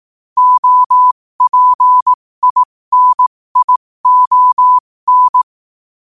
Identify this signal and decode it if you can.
Morse code, heard as a keyed tone, about 18 wpm, spaced as for 15 wpm, 1000 Hz. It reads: OPINION
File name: qotdfiles / 15wpm